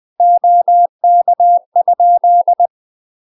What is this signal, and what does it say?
Morse code, heard as a keyed tone, about 20 wpm, 700 Hz. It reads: OK?
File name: UK MASTER QSOs TRAINING / 20WPM / UK MASTER QSOs 0023